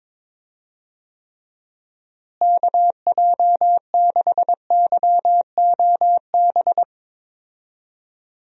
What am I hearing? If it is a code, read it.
Morse code, heard as a keyed tone, about 22 wpm, 700 Hz. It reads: KJ6YOB
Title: USA Call Signs - 6 Characters - Repeated 2x - 22wpm